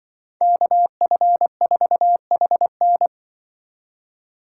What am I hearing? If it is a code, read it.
Morse code, heard as a keyed tone, about 24 wpm, 700 Hz. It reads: KF4HN
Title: Speed Racing - USA Call Signs 20wpm